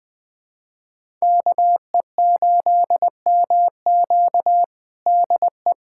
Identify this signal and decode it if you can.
Morse code, heard as a keyed tone, about 21 wpm, 700 Hz. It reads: KE8MQ DE
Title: USA Callsign DE Callsign 20wpm